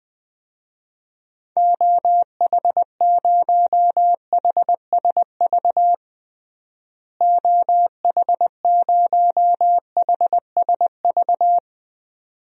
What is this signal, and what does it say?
Morse code, heard as a keyed tone, about 20 wpm, 700 Hz. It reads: OH0HSV OH0HSV
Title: DXCC-100 Call Signs - 6 Characters - Repeated 2x - 20wpm